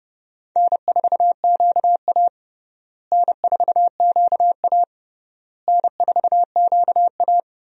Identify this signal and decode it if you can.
Morse code, heard as a keyed tone, about 30 wpm, 700 Hz. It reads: N4QA N4QA N4QA
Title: USA Call Signs - 4 Characters - Repeated 3x - 30wpm